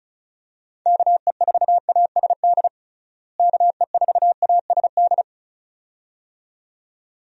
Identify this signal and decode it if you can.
Morse code, heard as a keyed tone, about 34 wpm, 700 Hz. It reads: KE4ASD KE4ASD
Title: USA Call Signs - 6 Characters - Repeated 2x - 35wpm